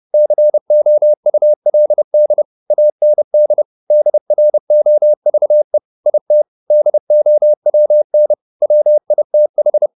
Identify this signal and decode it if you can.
Morse code, heard as a keyed tone, about 30 wpm, 600 Hz. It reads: COULD AND DROVE IT DOWN WITH